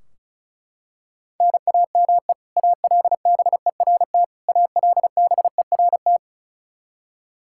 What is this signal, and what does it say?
Morse code, heard as a keyed tone, about 36 wpm, 700 Hz. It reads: NAME ALBERT ALBERT
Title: Practice QSO - Exchange 1 - Beginning - 35wpm